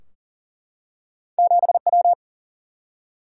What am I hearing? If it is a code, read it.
Morse code, heard as a keyed tone, about 40 wpm, 700 Hz. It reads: 7W
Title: Single-Letter Single-Number 40wpm